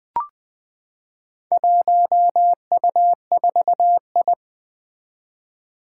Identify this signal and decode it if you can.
Morse code, heard as a keyed tone, about 21 wpm, 700 Hz. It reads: 1U4I
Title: Four Challenging Letter-Number Combinations 20wpm